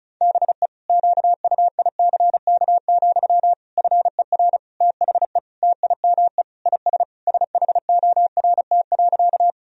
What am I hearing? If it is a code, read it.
Morse code, heard as a keyed tone, about 36 wpm, 700 Hz. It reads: BE QUICK, FER THE TIME IS SHORT.
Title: Sentences from Top 500 Words and Abbreviations 35wpm